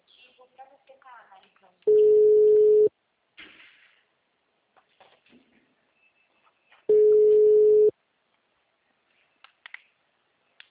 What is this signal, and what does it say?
Ringback tone (Europe, 425 Hz), with speech.